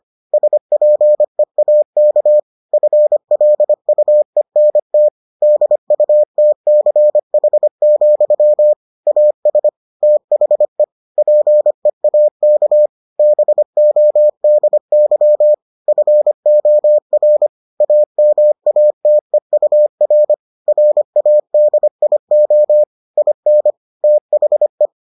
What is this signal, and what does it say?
Morse code, heard as a keyed tone, about 25 wpm, 600 Hz. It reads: SPEAK FLUENT DUTCH, AS THE PEAK BODY FOR AMATEUR RADIO IN THE